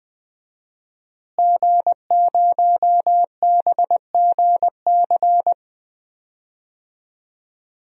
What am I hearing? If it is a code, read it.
Morse code, heard as a keyed tone, about 20 wpm, 700 Hz. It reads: G0BGC
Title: DXCC-100 Call Signs - 5 Characters - Repeated 2x - 20wpm